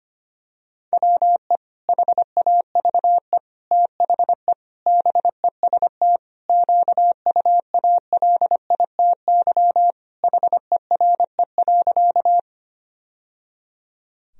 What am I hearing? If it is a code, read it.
Morse code, heard as a keyed tone, about 25 wpm, 700 Hz. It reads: WE HAVE THE BEST QUALITY HERE.
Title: Sentences from Top 1000 Words - Review Part 4 - 25wpm